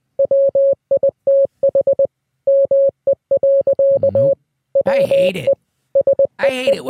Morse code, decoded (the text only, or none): WITH ME. WE SEE